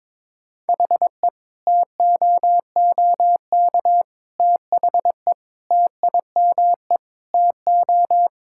Morse code, heard as a keyed tone, about 22 wpm, 700 Hz. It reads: HE TOOK THE TIME TO
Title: Sentences from Top 1600 Words 22wpm